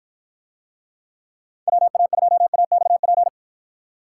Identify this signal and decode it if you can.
Morse code, heard as a keyed tone, about 50 wpm, 700 Hz. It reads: WA2AXP